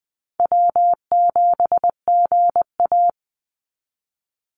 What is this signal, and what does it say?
Morse code, heard as a keyed tone, about 20 wpm, 700 Hz. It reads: W7GA
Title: NA SOTA Chaser Callsigns 20wpm